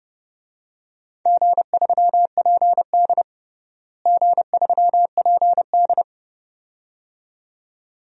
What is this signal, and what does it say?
Morse code, heard as a keyed tone, about 30 wpm, 700 Hz. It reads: G3PD G3PD
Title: DXCC-100 Call Signs - 4 Characters - Repeated 2x - 30wpm